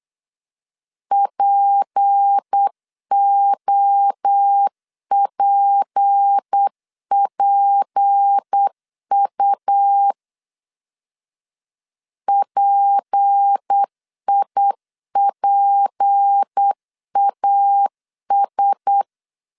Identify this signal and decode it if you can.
Morse code, heard as a keyed tone, about 8 wpm, 800 Hz. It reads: POPPU PIPAS
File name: LEZIONI / 6 PKX0 / 6 P 8wpm 3025